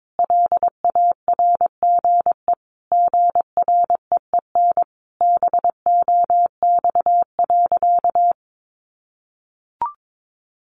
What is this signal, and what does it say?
Morse code, heard as a keyed tone, about 22 wpm, 700 Hz. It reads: LARGE GREEN BOX.
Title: Sentences from Top 500 Words 22wpm